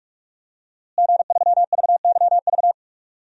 Morse code, heard as a keyed tone, about 46 wpm, 700 Hz. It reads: G3VYV
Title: DXCC-100 Call Signs - 5 Characters - Repeated 3x - 45wpm